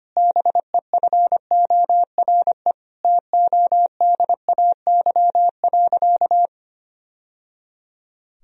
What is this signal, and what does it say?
Morse code, heard as a keyed tone, about 25 wpm, 700 Hz. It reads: BEFORE TODAY.
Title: Sentences from Top 1000 Words - Review Part 4 - 25wpm